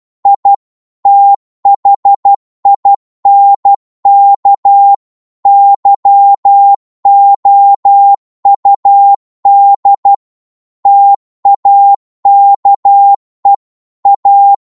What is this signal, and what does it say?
Morse code, heard as a keyed tone, about 12 wpm, 800 Hz. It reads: I THINK YOUD TAKE A
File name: Morse / psg-12wpm